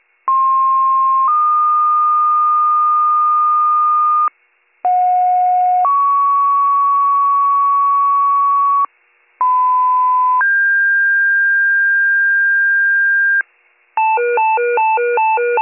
Two-tone page, heard as a one-second tone, then a three-second tone, about 1060 then 1220 Hz, about 730 then 1120 Hz, about 1010 then 1600 Hz. It is heard from 0.3 to 13.4 s.